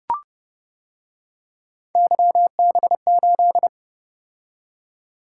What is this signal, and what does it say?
Morse code, heard as a keyed tone, about 29 wpm, 700 Hz. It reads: YB8